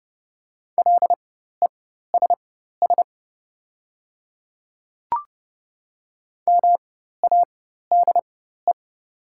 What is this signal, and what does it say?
Morse code, heard as a keyed tone, about 30 wpm, spaced as for 14 wpm, 700 Hz. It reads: LESS MADE
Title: 4x Character Spacing - Top 500 Words - 30wpm